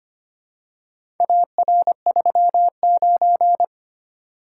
Morse code, heard as a keyed tone, about 25 wpm, 700 Hz. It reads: AR39
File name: practice / 25wpm / Four Letter-Number Combinations - 25wpm